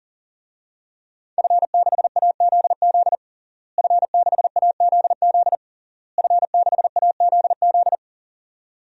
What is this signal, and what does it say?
Morse code, heard as a keyed tone, about 40 wpm, 700 Hz. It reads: F6AZZ F6AZZ F6AZZ
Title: DXCC-100 Call Signs - 5 Characters - Repeated 3x - 40wpm